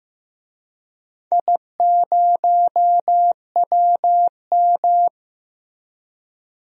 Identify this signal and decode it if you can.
Morse code, heard as a keyed tone, about 15 wpm, 700 Hz. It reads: I0WM
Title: DXCC-100 Call Signs - 4 Characters - 15wpm